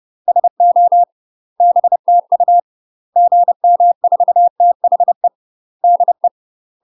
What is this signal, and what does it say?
Morse code, heard as a keyed tone, about 30 wpm, 700 Hz. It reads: SO BTU GM4THE DE
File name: UK MASTER QSOs TRAINING / 30WPM / UK MASTER QSOs 0027